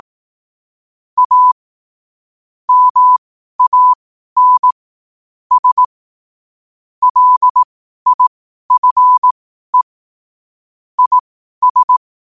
Morse code, heard as a keyed tone, about 18 wpm, spaced as for 12 wpm, 1000 Hz. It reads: A MAN S LIFE IS
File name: qotdfiles / 12wpm